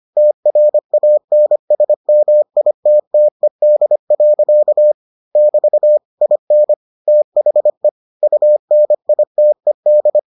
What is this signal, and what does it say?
Morse code, heard as a keyed tone, about 25 wpm, 600 Hz. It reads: TRANSMITTED. <BT> IN THE UNITED